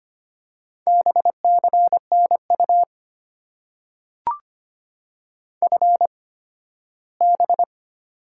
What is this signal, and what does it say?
Morse code, heard as a keyed tone, about 25 wpm, spaced as for 15 wpm, 690 Hz. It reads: BCNU FB
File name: Top 90 QSO Elements - 8x Character Spacing - 25wpm